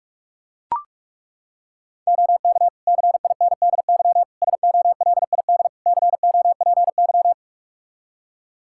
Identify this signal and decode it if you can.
Morse code, heard as a keyed tone, about 41 wpm, 700 Hz. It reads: OK CINDY SOLID COPY
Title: K1USN SST QSO Practice 45wpm